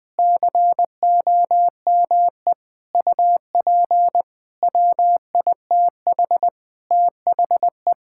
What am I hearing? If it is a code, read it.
Morse code, heard as a keyed tone, about 20 wpm, 700 Hz. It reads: COME UP WITH THE
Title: Sentences with 8-Word Phrases - 20wpm